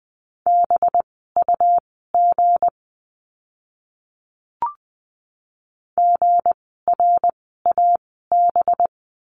Morse code, heard as a keyed tone, about 20 wpm, spaced as for 14 wpm, 700 Hz. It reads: BUG GRAB
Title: 185 Lesson 30f - Letter b - Words 20wpm